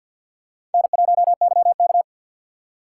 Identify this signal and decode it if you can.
Morse code, heard as a keyed tone, about 53 wpm, 700 Hz. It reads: N1YK
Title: USA Call Signs - 4 Characters - 3x Character Spacing - 50wpm